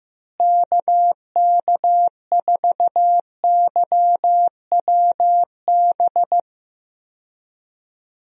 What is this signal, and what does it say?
Morse code, heard as a keyed tone, about 15 wpm, 700 Hz. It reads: KK4YWB
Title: USA Call Signs - 6 Characters - Repeated 3x - 15wpm